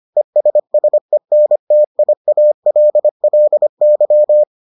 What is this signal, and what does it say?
Morse code, heard as a keyed tone, about 25 wpm, 600 Hz. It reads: ESSENTIALLY